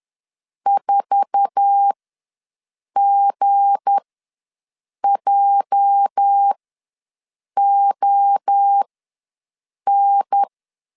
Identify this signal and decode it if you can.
Morse code, heard as a keyed tone, about 10 wpm, spaced as for 6 wpm, 800 Hz. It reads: 4GJON
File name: LEZIONI / 9 2 ws 10wpm 4130